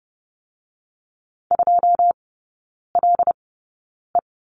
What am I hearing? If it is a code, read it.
Morse code, heard as a keyed tone, about 30 wpm, spaced as for 9 wpm, 690 Hz. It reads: 2LE
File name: Lesson 17e - Letter u - Three Characters 30wpm